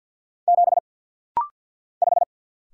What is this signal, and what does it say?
Morse code, heard as a keyed tone, about 50 wpm, 700 Hz. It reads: 7 5
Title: Single Letter-Number - Rapid-Fire 50wpm